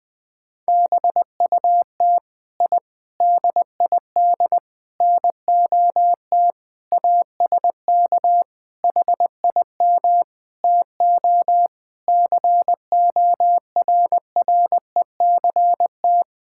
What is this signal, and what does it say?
Morse code, heard as a keyed tone, about 21 wpm, 700 Hz. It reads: BUT I DID NOT ASK HIM TO CORRECT
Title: Sentences from Top 1000 Words - Review Part 5 - 20wpm